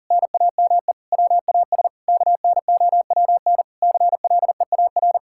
Morse code, heard as a keyed tone, about 40 wpm, 700 Hz. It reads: NAME WAS KNOWN CLEAR
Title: Sentences from Top 1000 Words - Review Part 5 - 40wpm